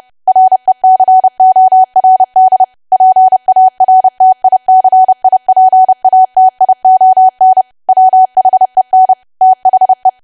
Morse code, heard as a keyed tone, about 30 wpm, 750 Hz. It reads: RECORD PARTICIPATION WHEN THE
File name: cwa / W1AW-Level3 / 30WPM